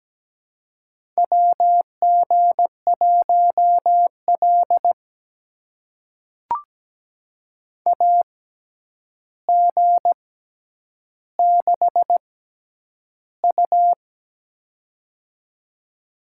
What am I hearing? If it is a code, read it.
Morse code, heard as a keyed tone, about 17 wpm, spaced as for 9 wpm, 700 Hz. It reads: WG1L AG6U
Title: USA Call Signs - 4 Characters - 6x Character Spacing - 17wpm